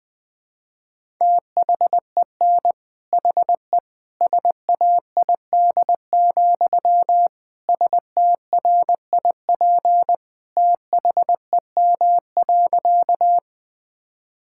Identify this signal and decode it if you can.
Morse code, heard as a keyed tone, about 21 wpm, 700 Hz. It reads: THEN HE SAID, STRIP THEM.
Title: Sentences from Top 1600 Words 20wpm